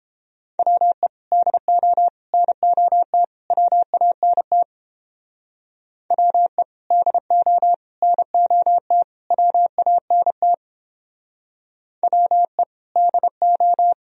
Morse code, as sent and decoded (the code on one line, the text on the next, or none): .-- . / -.. --- / -. --- - / .-- .- -. - / .-- . / -.. --- / -. --- - / .-- .- -. - / .-- . / -.. ---
WE DO NOT WANT WE DO NOT WANT WE DO